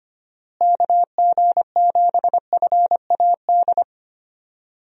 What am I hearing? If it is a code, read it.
Morse code, heard as a keyed tone, about 25 wpm, 700 Hz. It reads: KG7FAD
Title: USA Call Signs - 6 Characters - Repeated 2x - 25wpm